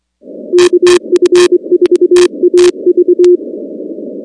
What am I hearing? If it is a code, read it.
Morse code, heard as a keyed tone, about 24 wpm, 360 Hz. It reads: KF4AV
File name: contests / 2019-cw-wpx-kt5j-k5pi- -2330 0-19